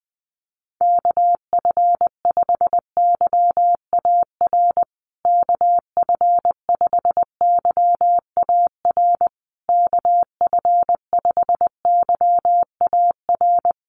Morse code, heard as a keyed tone, about 20 wpm, 700 Hz. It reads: KF5YAR KF5YAR KF5YAR
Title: USA Call Signs Repeated 3x 20wpm